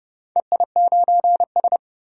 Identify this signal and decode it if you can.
Morse code, heard as a keyed tone, about 30 wpm, 700 Hz. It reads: EI9S
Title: DXCC-100 Call Signs - 4 Characters - 30wpm